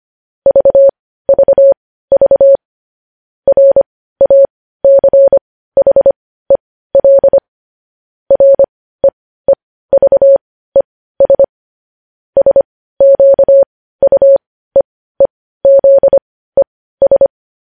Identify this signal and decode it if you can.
Morse code, heard as a keyed tone, about 25 wpm, spaced as for 15 wpm, 550 Hz. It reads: VVV RACHEL REEVES SQUEEZES
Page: News Headlines in Morse Code at 15 WPM